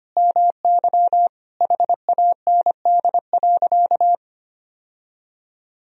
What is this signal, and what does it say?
Morse code, heard as a keyed tone, about 25 wpm, 700 Hz. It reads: MY HAND.
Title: Sentences from Top 1000 Words - Review Part 1 - 25wpm